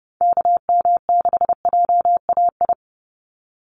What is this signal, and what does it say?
Morse code, heard as a keyed tone, about 30 wpm, 700 Hz. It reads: KM6JAI
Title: USA Call Signs 30wpm